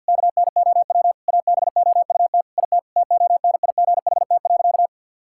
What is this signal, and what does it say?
Morse code, heard as a keyed tone, about 52 wpm, 700 Hz. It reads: KNOW ABOUT IT TONIGHT.
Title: Sentences from Top 1000 Words - Review Part 5 - 50wpm